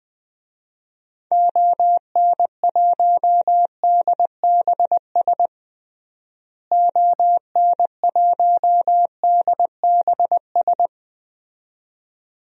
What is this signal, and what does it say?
Morse code, heard as a keyed tone, about 20 wpm, 700 Hz. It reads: ON1DBS ON1DBS
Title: DXCC-100 Call Signs - 6 Characters - Repeated 2x - 20wpm